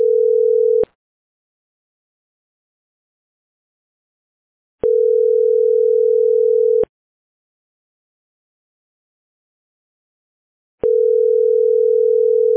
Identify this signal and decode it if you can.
Ringback tone (North America), 440 + 480 Hz.